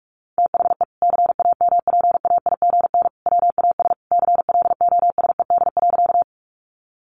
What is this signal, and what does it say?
Morse code, heard as a keyed tone, about 44 wpm, 700 Hz. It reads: THE CAMPAIGN WAS CLOSED.
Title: Sentences from Top 1400 Words 45wpm